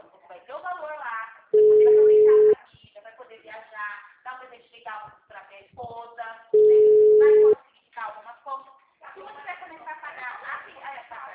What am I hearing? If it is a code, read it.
Ringback tone (Europe, 425 Hz), with speech.
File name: out- -2021- - - .1338